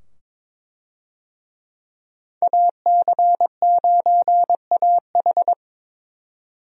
Morse code, heard as a keyed tone, about 22 wpm, 700 Hz. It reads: AC9AH
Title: USA Call Signs - 5 Characters - 22wpm